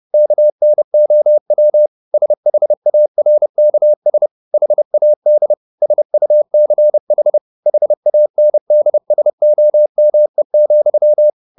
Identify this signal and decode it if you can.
Morse code, heard as a keyed tone, about 30 wpm, 600 Hz. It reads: KNOW SHARKS HAD SUCH HANDSOME,